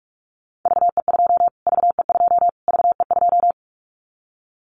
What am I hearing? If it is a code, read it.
Morse code, heard as a keyed tone, about 44 wpm, 700 Hz. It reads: VE2 VE2 VE2
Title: Callsign Prefixes for Top 25 DXCC - Repeated 3x - 45wpm